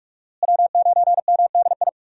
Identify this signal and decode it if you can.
Morse code, heard as a keyed tone, about 43 wpm, 700 Hz. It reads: W9MDI
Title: USA Call Signs - 5 Characters - Repeated 3x - 45wpm